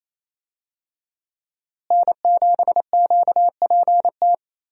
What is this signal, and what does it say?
Morse code, heard as a keyed tone, about 28 wpm, 700 Hz. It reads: N7QPT
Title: USA Call Signs - 5 Characters - Repeated 3x - 28wpm